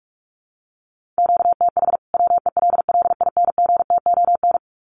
Morse code, heard as a keyed tone, about 44 wpm, 700 Hz. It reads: QTH WELLINGTON
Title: Common QSO Phrases 45wpm